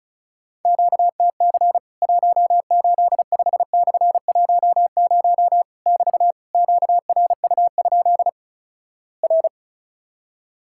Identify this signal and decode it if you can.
Morse code, heard as a keyed tone, about 36 wpm, 700 Hz. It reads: QTC 185/10 <BT> QRU? R
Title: Worked All Europe DX Contest - QTC Exchange - 35wpm